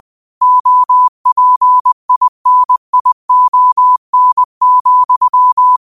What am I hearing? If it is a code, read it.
Morse code, heard as a keyed tone, about 20 wpm, 1000 Hz. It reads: OPINION,
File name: qotdfiles / 20wpm